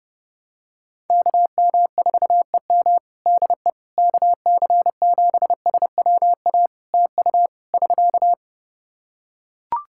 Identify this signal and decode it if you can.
Morse code, heard as a keyed tone, about 30 wpm, 700 Hz. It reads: KM4EM DE KC7SWA TU <SK>
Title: Practice QSO - Exchange 4 - Ending - 30wpm